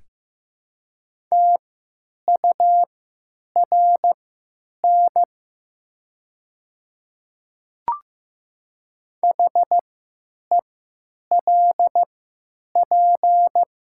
Morse code, heard as a keyed tone, about 15 wpm, spaced as for 9 wpm, 700 Hz. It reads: TURN HELP
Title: 3x Character Spacing - Top 200 Words - 15wpm